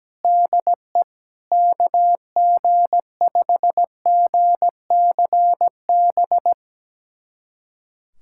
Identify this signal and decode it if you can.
Morse code, heard as a keyed tone, about 17 wpm, 700 Hz. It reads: DE KG5GCB
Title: USA Callsign DE Callsign 17wpm